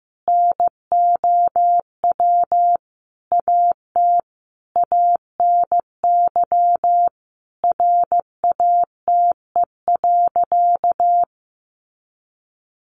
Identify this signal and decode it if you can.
Morse code, heard as a keyed tone, about 15 wpm, 700 Hz. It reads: NOW AT ANY RATE.
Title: Sentences from Top 900 Words 15wpm